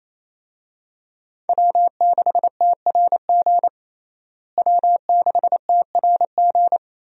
Morse code, heard as a keyed tone, about 28 wpm, 700 Hz. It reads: W6TRG W6TRG
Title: USA Call Signs - 5 Characters - Repeated 2x - 28wpm